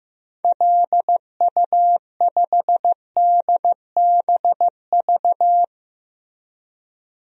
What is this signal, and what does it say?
Morse code, heard as a keyed tone, about 16 wpm, 700 Hz. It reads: LU5DBV
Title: DXCC-100 Call Signs - 6 Characters - Repeated 3x - 15wpm